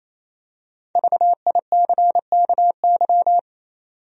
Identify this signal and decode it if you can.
Morse code, heard as a keyed tone, about 28 wpm, 700 Hz. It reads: VICKY
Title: USA Common Names - 28wpm